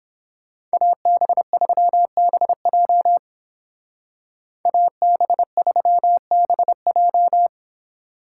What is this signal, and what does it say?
Morse code, heard as a keyed tone, about 28 wpm, 700 Hz. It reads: AB3BJ AB3BJ